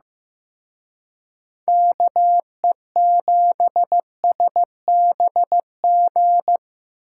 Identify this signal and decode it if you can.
Morse code, heard as a keyed tone, about 18 wpm, 700 Hz. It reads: KE7SBG
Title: Speed Racing - USA 2x3 Call Signs 15wpm